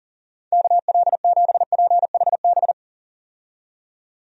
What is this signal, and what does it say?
Morse code, heard as a keyed tone, about 40 wpm, 700 Hz. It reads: KL7PHB
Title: DXCC-100 Call Signs - 6 Characters - 40wpm